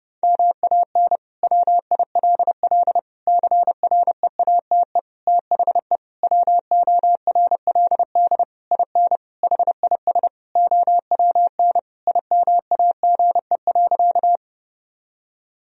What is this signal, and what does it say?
Morse code, heard as a keyed tone, about 30 wpm, 700 Hz. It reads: MAN WILL CREATE THE WORLD IN HIS OWN IMAGE.
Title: Sentences from Top 1000 Words - Review Part 4 - 30wpm